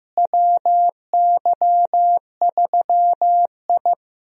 Morse code, heard as a keyed tone, about 16 wpm, 700 Hz. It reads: WY3I
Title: USA Call Signs - 4 Characters - Repeated 3x - 15wpm